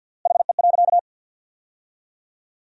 Morse code, heard as a keyed tone, about 53 wpm, 700 Hz. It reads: HE.
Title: 3x Word-Spacing - Sentences from Top 100 Words - 50wpm